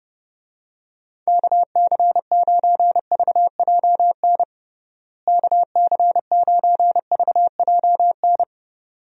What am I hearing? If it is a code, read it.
Morse code, heard as a keyed tone, about 32 wpm, 700 Hz. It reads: KC9VJN KC9VJN
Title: USA Call Signs - 6 Characters - Repeated 2x - 30wpm